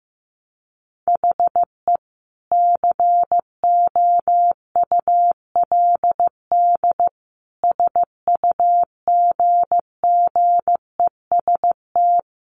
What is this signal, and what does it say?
Morse code, heard as a keyed tone, about 15 wpm, 700 Hz. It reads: HE COULD SUGGEST
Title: Sentences from Top 1300 Words 15wpm